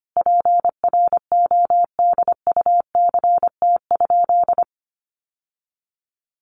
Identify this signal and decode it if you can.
Morse code, heard as a keyed tone, about 25 wpm, 700 Hz. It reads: PRODUCT?
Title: Sentences from Top 1400 Words 25wpm